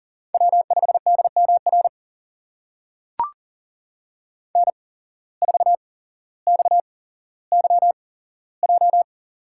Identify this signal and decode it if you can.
Morse code, heard as a keyed tone, about 40 wpm, spaced as for 17 wpm, 700 Hz. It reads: W5DMR N4XYJ
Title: USA Call Signs - 5 Characters - 8x Character Spacing - 40wpm